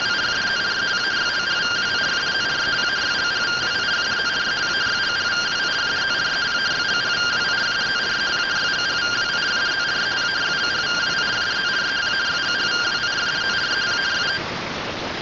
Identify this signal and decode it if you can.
Two-tone teleprinter signal, about 170 Hz shift.
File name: findings / AXIAL fr130 3mm 409.gen-29 rtty rx